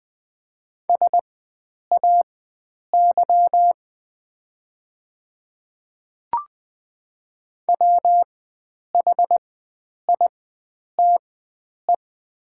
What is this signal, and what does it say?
Morse code, heard as a keyed tone, about 21 wpm, spaced as for 9 wpm, 700 Hz. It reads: SAY WHITE
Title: 4x Character Spacing - Top 500 Words - 20wpm